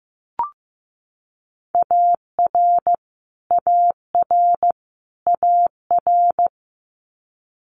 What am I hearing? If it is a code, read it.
Morse code, heard as a keyed tone, about 15 wpm, 700 Hz. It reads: AR AR AR
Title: US States Repeated 3x 15wpm